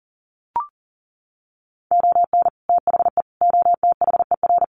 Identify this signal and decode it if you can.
Morse code, heard as a keyed tone, about 40 wpm, 700 Hz. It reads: ON THE OTHER
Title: Sets of 4 Words 40wpm